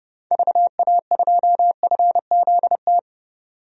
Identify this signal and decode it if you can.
Morse code, heard as a keyed tone, about 31 wpm, 700 Hz. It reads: VA2FZT
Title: DXCC-100 Call Signs - 6 Characters - 30wpm